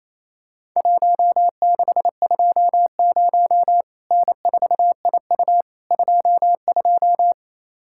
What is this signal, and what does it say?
Morse code, heard as a keyed tone, about 28 wpm, 700 Hz. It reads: 1620 N4IU 22
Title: Worked All Europe DX Contest - QTC Exchange - 28wpm